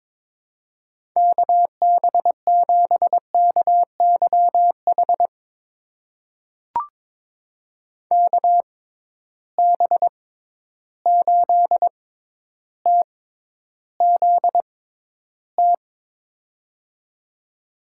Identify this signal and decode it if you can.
Morse code, heard as a keyed tone, about 22 wpm, spaced as for 11 wpm, 700 Hz. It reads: KB7KYH KB8TZT
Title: USA Call Signs - 6 Characters - 6x Character Spacing - 22wpm